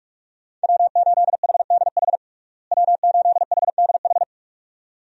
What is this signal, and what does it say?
Morse code, heard as a keyed tone, about 46 wpm, 700 Hz. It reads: W8HDH W8HDH
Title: USA Call Signs - 5 Characters - Repeated 2x - 45wpm